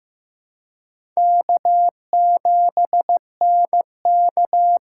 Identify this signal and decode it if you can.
Morse code, heard as a keyed tone, about 15 wpm, 700 Hz. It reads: K7NK
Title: USA Call Signs - 4 Characters - 2x Character Spacing - 15wpm